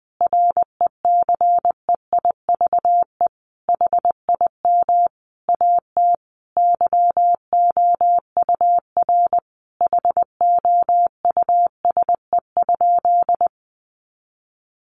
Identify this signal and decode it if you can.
Morse code, heard as a keyed tone, about 20 wpm, 700 Hz. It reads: RECEIVE HIM AT YOUR HOUSE?
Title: Sentences from Top 1000 Words 20wpm